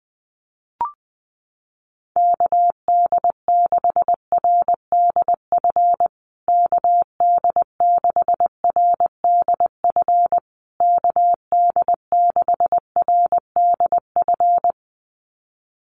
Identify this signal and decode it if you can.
Morse code, heard as a keyed tone, about 20 wpm, 700 Hz. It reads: KD6RDF KD6RDF KD6RDF
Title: USA Call Signs Repeated 3x 20wpm